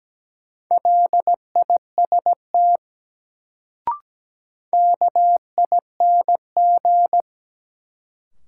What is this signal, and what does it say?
Morse code, heard as a keyed tone, about 17 wpm, 700 Hz. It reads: LIST KING
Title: Sentences from Top 400 Words - New Word Drill - 17wpm